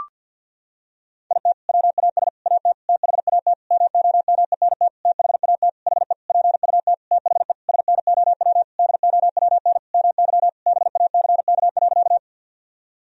Morse code, heard as a keyed tone, about 50 wpm, 700 Hz. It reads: IT WAS AT THAT MOMENT THAT HE PUT THE SNOW DOWN MY BACK.
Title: Sentences from Top 1000 Words - Review Part 3 - 50wpm